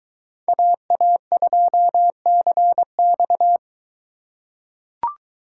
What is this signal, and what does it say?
Morse code, heard as a keyed tone, about 22 wpm, 700 Hz. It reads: AA2CX